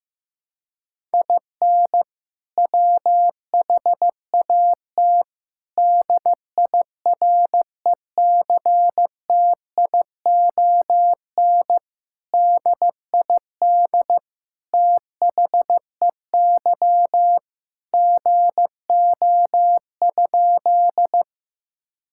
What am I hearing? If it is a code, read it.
Morse code, heard as a keyed tone, about 15 wpm, 700 Hz. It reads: IN WHAT DIRECTION DID THEY GO?
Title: Sentences from Top 1000 Words - Review Part 5 - 15wpm